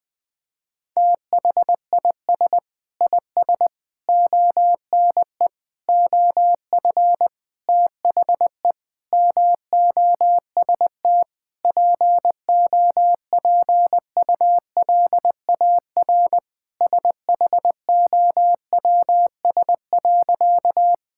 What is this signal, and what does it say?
Morse code, heard as a keyed tone, about 20 wpm, 700 Hz. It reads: THIS IS ONE OF THE MOST POPULAR SHOWS.
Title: Sentences with 5-Word Phrases - 20wpm